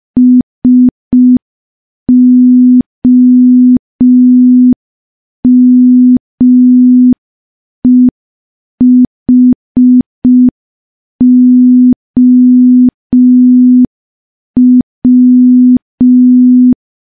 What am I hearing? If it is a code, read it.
Morse code, heard as a keyed tone, about 5 wpm, 250 Hz. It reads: SOMEHOW
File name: morse / morse-training-05WPM